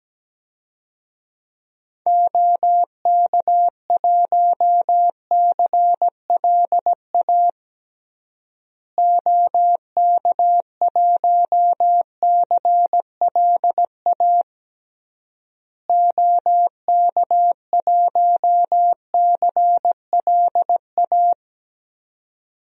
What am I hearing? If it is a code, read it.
Morse code, heard as a keyed tone, about 17 wpm, 700 Hz. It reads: OK1CLA OK1CLA OK1CLA
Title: DXCC-100 Call Signs - 6 Characters - Repeated 3x - 17wpm